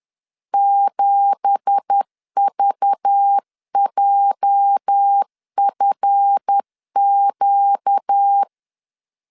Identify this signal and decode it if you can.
Morse code, heard as a keyed tone, about 10 wpm, 800 Hz. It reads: 7VJFQ
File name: LEZIONI / 9 2867 10wpm 2733